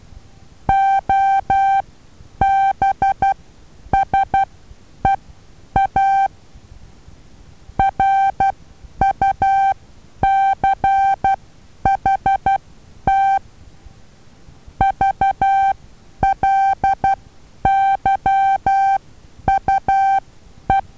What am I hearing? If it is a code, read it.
Morse code, heard as a keyed tone, about 12 wpm, spaced as for 9 wpm, 780 Hz. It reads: OBSEA RUCHT VLYUE
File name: Taviro HA6VA B